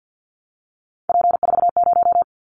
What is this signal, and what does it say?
Morse code, heard as a keyed tone, about 50 wpm, 700 Hz. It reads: P40
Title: Callsign Prefixes for Top 100 DXCC - 50wpm